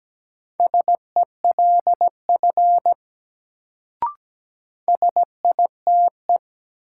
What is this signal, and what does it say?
Morse code, heard as a keyed tone, about 17 wpm, 700 Hz. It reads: SELF SITE
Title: Sentences from Top 1000 Words - Review Part 1 - New Word Drill - 17wpm